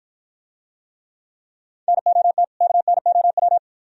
Morse code, heard as a keyed tone, about 53 wpm, 700 Hz. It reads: NOT KNOW